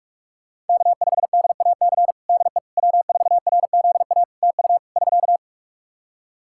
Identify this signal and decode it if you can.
Morse code, heard as a keyed tone, about 46 wpm, 700 Hz. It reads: K5DAC DE W4RZA TU <SK>
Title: Practice QSO - Exchange 4 - Ending - 45wpm